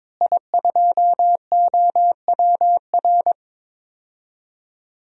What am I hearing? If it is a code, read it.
Morse code, heard as a keyed tone, about 22 wpm, 700 Hz. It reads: I2OWR
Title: DXCC-100 Call Signs - 5 Characters - Repeated 2x - 22wpm